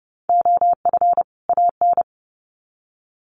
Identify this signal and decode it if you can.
Morse code, heard as a keyed tone, about 30 wpm, 700 Hz. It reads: OF AN
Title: Sets of 2 Words 30wpm